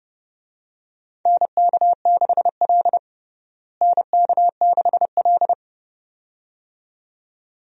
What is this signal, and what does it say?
Morse code, heard as a keyed tone, about 32 wpm, 700 Hz. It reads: NK6L NK6L
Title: USA Call Signs - 4 Characters - Repeated 2x - 30wpm